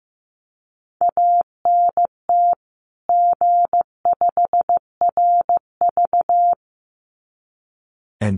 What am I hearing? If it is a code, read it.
Morse code, heard as a keyed tone, about 15 wpm, 700 Hz. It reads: ANT G5RV
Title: Common QSO Phrases 15wpm